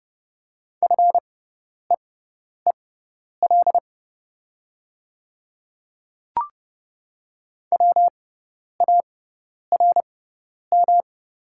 Morse code, heard as a keyed tone, about 30 wpm, spaced as for 10 wpm, 700 Hz. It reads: FEEL WARM
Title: 6x Character Spacing - Top 500 Words - 30wpm